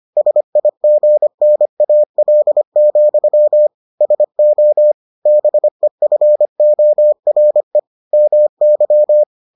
Morse code, heard as a keyed tone, about 25 wpm, 600 Hz. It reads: SIGNAL, SO BEFORE MY